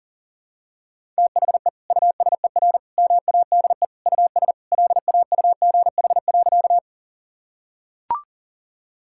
Morse code, heard as a keyed tone, about 40 wpm, 700 Hz. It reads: THE USER MADE US LAUGH.
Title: Sentences from Top 1000 Words - Review Part 5 - 40wpm